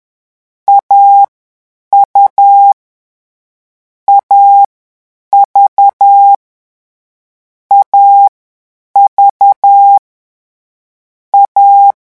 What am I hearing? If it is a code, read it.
Morse code, heard as a keyed tone, about 10 wpm, 800 Hz. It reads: A U A V A V A
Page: Leçon 3 - vitesse 10 mots minute